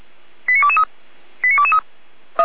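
Five-tone selective call (ZVEI), heard as quick runs of five tones, tones of about 70 ms. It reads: 89223 89223
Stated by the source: ZVEI five-tone selective call received by SDR.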